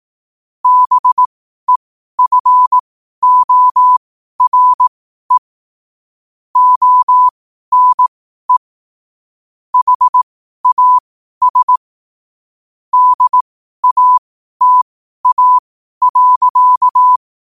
Morse code, heard as a keyed tone, about 18 wpm, spaced as for 12 wpm, 1000 Hz. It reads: BEFORE ONE HAS DATA.
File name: qotdfiles / 12wpm